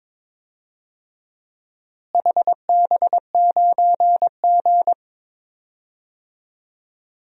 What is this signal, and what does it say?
Morse code, heard as a keyed tone, about 21 wpm, 700 Hz. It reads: HB9G
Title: DXCC-100 Call Signs - 4 Characters - Repeated 2x - 22wpm